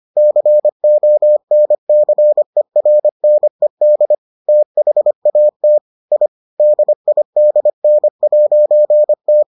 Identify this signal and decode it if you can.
Morse code, heard as a keyed tone, about 25 wpm, 600 Hz. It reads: CONCERNED THAT I DIDN'T